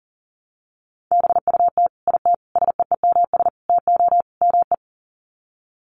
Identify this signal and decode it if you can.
Morse code, heard as a keyed tone, about 40 wpm, 700 Hz. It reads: BUT IT SEEMS TO ME
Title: Sets of 5 Words Encore2 40wpm